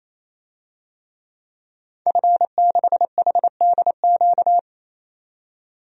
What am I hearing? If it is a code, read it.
Morse code, heard as a keyed tone, about 28 wpm, 700 Hz. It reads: F6HDQ
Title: DXCC-100 Call Signs - 5 Characters - 28wpm